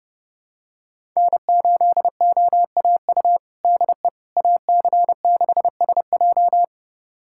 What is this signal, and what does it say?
Morse code, heard as a keyed tone, about 30 wpm, 700 Hz. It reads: N8OAU DE AC6SJ
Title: USA Callsign DE Callsign - 5 Characters - 30wpm